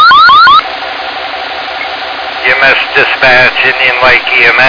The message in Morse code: none

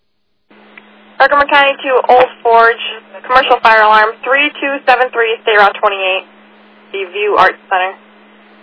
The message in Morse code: none

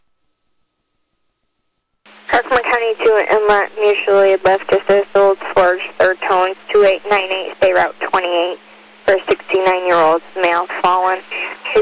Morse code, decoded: none

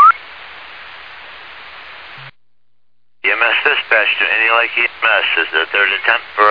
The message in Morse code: none

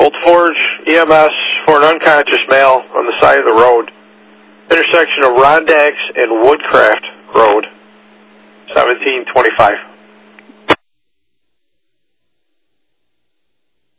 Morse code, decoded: none

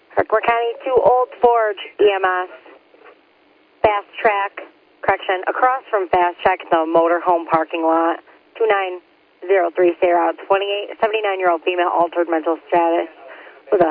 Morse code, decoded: none

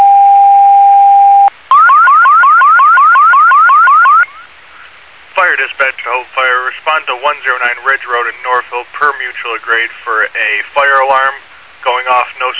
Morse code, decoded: none

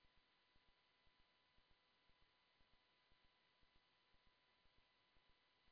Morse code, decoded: none

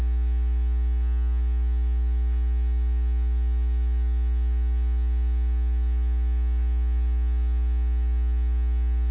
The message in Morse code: none

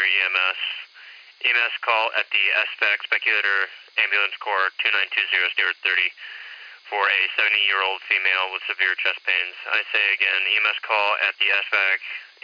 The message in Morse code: none

